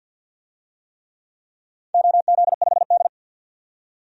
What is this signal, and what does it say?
Morse code, heard as a keyed tone, about 50 wpm, 700 Hz. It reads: OZ5D
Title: DXCC-100 Call Signs - 4 Characters - 50wpm